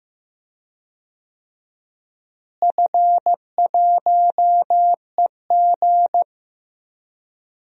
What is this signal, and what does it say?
Morse code, heard as a keyed tone, about 15 wpm, 700 Hz. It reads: F1EG
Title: DXCC-100 Call Signs - 4 Characters - Repeated 2x - 15wpm